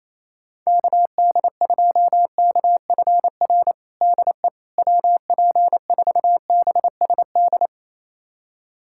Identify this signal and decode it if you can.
Morse code, heard as a keyed tone, about 28 wpm, 700 Hz. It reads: KD2KFR DE WP4BSD
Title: USA Callsign DE Callsign 28wpm